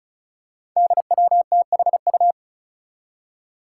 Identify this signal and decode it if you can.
Morse code, heard as a keyed tone, about 34 wpm, 700 Hz. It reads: DWTHU
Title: Five Challenging Letter-Number Combinations 35wpm